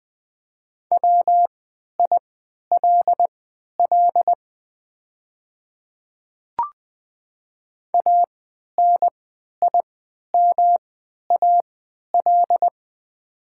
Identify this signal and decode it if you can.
Morse code, heard as a keyed tone, about 20 wpm, spaced as for 11 wpm, 700 Hz. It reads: WILL ANIMAL
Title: 3x Character Spacing - Top 200 Words - 20wpm